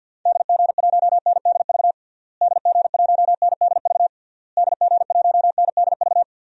Morse code, heard as a keyed tone, about 52 wpm, 700 Hz. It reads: DG1NDV DG1NDV DG1NDV
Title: DXCC-100 Call Signs - 6 Characters - Repeated 3x - 50wpm